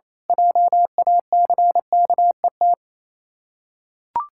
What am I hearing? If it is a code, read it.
Morse code, heard as a keyed tone, about 27 wpm, 690 Hz. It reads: JACKET
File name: Easy Words - Speed-Racing with Hints - 22wpm